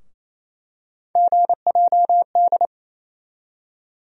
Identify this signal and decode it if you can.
Morse code, heard as a keyed tone, about 28 wpm, 700 Hz. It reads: GJD
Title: USA Call Sign Postfixes - 3 Letters 28wpm